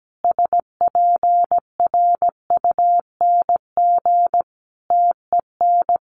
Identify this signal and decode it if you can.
Morse code, heard as a keyed tone, about 17 wpm, 700 Hz. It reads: SPRUNG TEN
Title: A Sound Of Thunder 17wpm